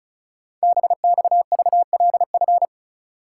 Morse code, heard as a keyed tone, about 36 wpm, 700 Hz. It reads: BXVLF
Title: Code Groups - 5 Letters - 35wpm